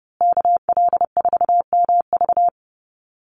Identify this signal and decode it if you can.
Morse code, heard as a keyed tone, about 30 wpm, 700 Hz. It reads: KL4MV
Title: USA Call Signs Repeated 2x 30wpm